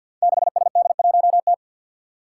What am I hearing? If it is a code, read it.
Morse code, heard as a keyed tone, about 50 wpm, 700 Hz. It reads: 6SD1T
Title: Five Challenging Letter-Number Combinations 50wpm